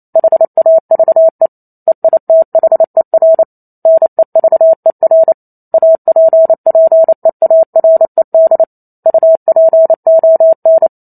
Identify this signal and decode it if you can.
Morse code, heard as a keyed tone, about 28 wpm, 650 Hz. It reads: HAVE EITHER NEVER APPEARED UPON